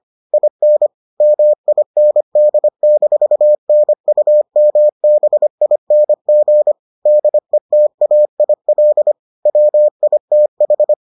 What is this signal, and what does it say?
Morse code, heard as a keyed tone, about 25 wpm, 600 Hz. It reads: IN MIND-NUMBING DETAIL WITH